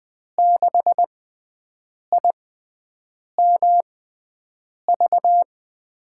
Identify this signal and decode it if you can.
Morse code, heard as a keyed tone, about 21 wpm, spaced as for 7 wpm, 700 Hz. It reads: 6IMV
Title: USA Call Signs - 6 Characters - 6x Character Spacing - 20wpm